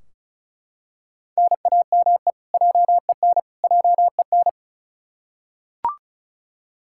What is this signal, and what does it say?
Morse code, heard as a keyed tone, about 36 wpm, 700 Hz. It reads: NAME JEN JEN
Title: Practice QSO - Exchange 1 - Beginning - 35wpm